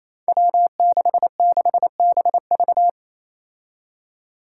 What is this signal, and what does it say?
Morse code, heard as a keyed tone, about 27 wpm, 700 Hz. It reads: W66BV
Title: Five Challenging Letter-Number Combinations 28wpm